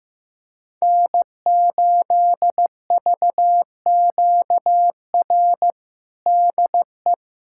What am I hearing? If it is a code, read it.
Morse code, heard as a keyed tone, about 16 wpm, 700 Hz. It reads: N8VQR DE
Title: USA Callsign DE Callsign - 5 Characters - 15wpm